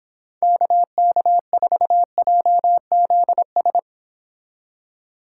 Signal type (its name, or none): Morse code